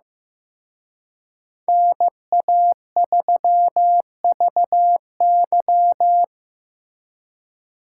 Morse code, heard as a keyed tone, about 18 wpm, 700 Hz. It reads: NA3VY
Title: Speed Racing - USA 2x2 Call Signs 15wpm